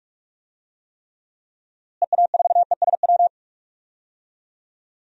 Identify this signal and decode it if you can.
Morse code, heard as a keyed tone, about 46 wpm, 700 Hz. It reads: EA4ESW
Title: DXCC-100 Call Signs - 6 Characters - Repeated 2x - 45wpm